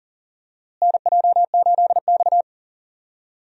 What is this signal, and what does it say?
Morse code, heard as a keyed tone, about 40 wpm, 700 Hz. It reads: NJ8X
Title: USA Call Signs - 4 Characters - Repeated 2x - 40wpm